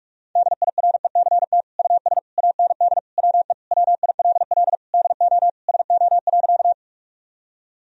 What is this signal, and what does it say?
Morse code, heard as a keyed tone, about 46 wpm, 700 Hz. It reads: DIRECT US AND WE WILL DO SO.
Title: Sentences from Top 1000 Words - Review Part 2 - 45wpm